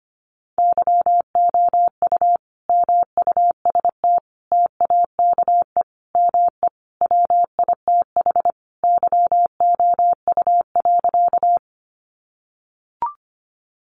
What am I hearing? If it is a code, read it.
Morse code, heard as a keyed tone, about 25 wpm, 700 Hz. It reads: YOU MUST TAKE ME WITH YOU.
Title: Sentences from Top 200 Words 25wpm